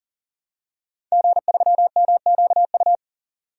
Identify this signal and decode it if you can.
Morse code, heard as a keyed tone, about 37 wpm, 700 Hz. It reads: G3MQU
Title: DXCC-100 Call Signs - 5 Characters - 40wpm